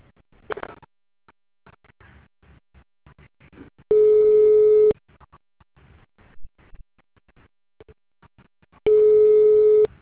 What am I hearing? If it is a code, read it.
Ringback tone (Europe, 425 Hz).